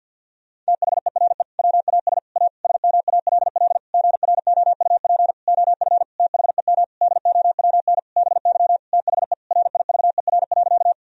Morse code, heard as a keyed tone, about 50 wpm, 700 Hz. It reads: THERE WAS A SMALL GROUP OF THEM DOWN BY THE RIVER.
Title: Sentences from Top 1000 Words - Review Part 1 - 50wpm